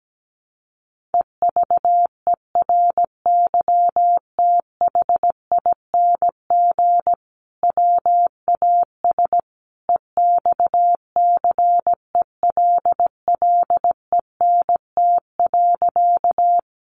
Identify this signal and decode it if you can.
Morse code, heard as a keyed tone, about 17 wpm, 700 Hz. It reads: EVERYTHING WAS EXCELLENT.
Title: Sentences from Top 1000 Words 17wpm